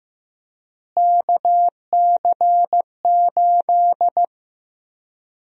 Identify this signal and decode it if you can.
Morse code, heard as a keyed tone, about 15 wpm, 700 Hz. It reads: KC8
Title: USA Call Sign Prefixes - Repeated 2x - 15wpm